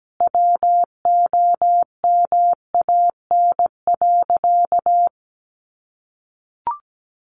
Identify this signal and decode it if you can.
Morse code, heard as a keyed tone, about 17 wpm, 700 Hz. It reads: WOMAN.
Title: Sentences from Top 1100 Words 17wpm